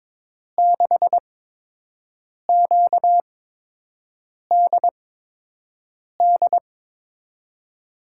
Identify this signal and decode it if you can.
Morse code, heard as a keyed tone, about 22 wpm, spaced as for 6 wpm, 700 Hz. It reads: 6QDD
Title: USA Call Signs - 6 Characters - 8x Character Spacing - 22wpm